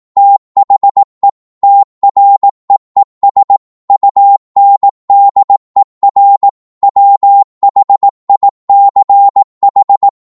Morse code, heard as a keyed tone, about 18 wpm, 800 Hz. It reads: THE TREES UNDER WHICH
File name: 18wpm